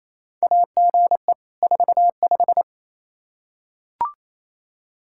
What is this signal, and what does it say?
Morse code, heard as a keyed tone, about 28 wpm, 700 Hz. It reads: AGE 45
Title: Practice QSO - Exchange 3 - More Info - 28wpm